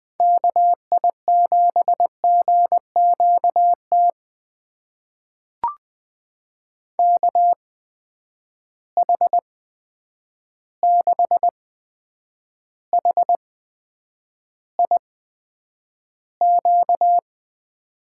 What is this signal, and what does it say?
Morse code, heard as a keyed tone, about 20 wpm, spaced as for 9 wpm, 700 Hz. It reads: KI7GQT KH6HIQ
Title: USA Call Signs - 6 Characters - 8x Character Spacing - 20wpm